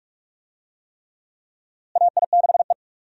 Morse code, heard as a keyed tone, about 46 wpm, 700 Hz. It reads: AI6E
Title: USA Call Signs - 4 Characters - Repeated 3x - 45wpm